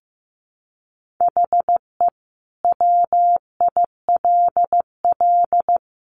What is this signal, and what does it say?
Morse code, heard as a keyed tone, about 15 wpm, 700 Hz. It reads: HE WILL
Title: Sentences from Top 1000 Words 15wpm